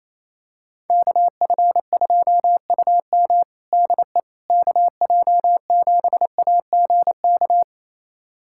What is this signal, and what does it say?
Morse code, heard as a keyed tone, about 28 wpm, 700 Hz. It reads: KF2UM DE KJ7AGK
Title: USA Callsign DE Callsign 28wpm